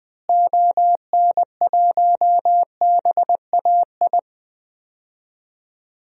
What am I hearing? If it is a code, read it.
Morse code, heard as a keyed tone, about 20 wpm, 700 Hz. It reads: ON1BAI
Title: DXCC-100 Call Signs - 6 Characters - Repeated 2x - 20wpm